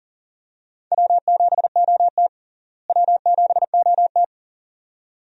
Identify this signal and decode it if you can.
Morse code, heard as a keyed tone, about 38 wpm, 700 Hz. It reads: W7OT W7OT
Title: USA Call Signs - 4 Characters - Repeated 2x - 40wpm